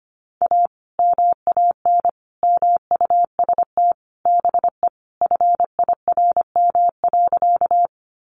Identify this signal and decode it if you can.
Morse code, heard as a keyed tone, about 25 wpm, 700 Hz. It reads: A MAN MUST BE FIRM.
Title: Sentences from Top 1400 Words 25wpm